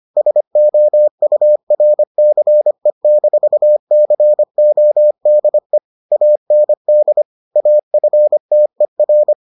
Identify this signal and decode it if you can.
Morse code, heard as a keyed tone, about 25 wpm, 600 Hz. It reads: SOURCE-CODE AND AFTER